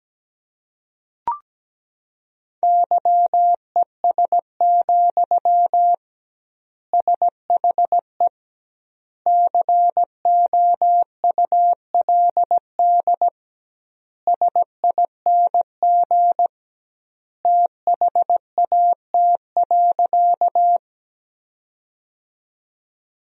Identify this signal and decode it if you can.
Morse code, heard as a keyed tone, about 17 wpm, 700 Hz. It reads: YES, SHE COULD SING THAT.
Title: 2x Word-Spacing - Sentences from Top 500 Words - 17wpm